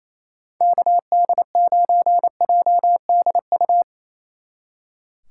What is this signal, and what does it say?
Morse code, heard as a keyed tone, about 36 wpm, 700 Hz. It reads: KD9JDU